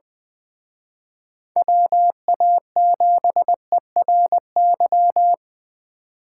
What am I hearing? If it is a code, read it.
Morse code, heard as a keyed tone, about 20 wpm, 700 Hz. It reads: WA7ERY
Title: Speed Racing - USA 2x3 Call Signs 17wpm